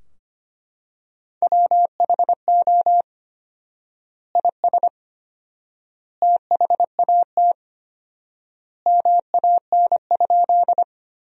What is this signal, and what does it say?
Morse code, heard as a keyed tone, about 25 wpm, 700 Hz. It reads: WHO IS THAT MAN?
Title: 4x Word-Spacing - Sentences from Top 200 Words - 25wpm